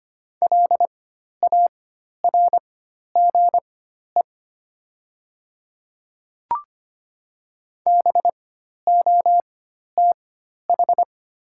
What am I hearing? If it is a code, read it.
Morse code, heard as a keyed tone, about 25 wpm, spaced as for 12 wpm, 700 Hz. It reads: LARGE BOTH
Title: 4x Character Spacing - Top 500 Words - 25wpm